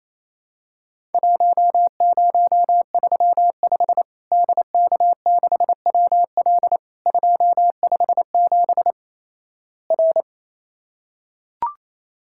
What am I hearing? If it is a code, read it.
Morse code, heard as a keyed tone, about 28 wpm, 700 Hz. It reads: 1035 DK6WL 257 R
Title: Worked All Europe DX Contest - QTC Exchange - 28wpm